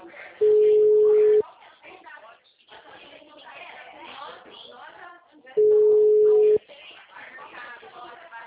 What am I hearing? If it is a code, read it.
Ringback tone (Europe, 425 Hz), with speech.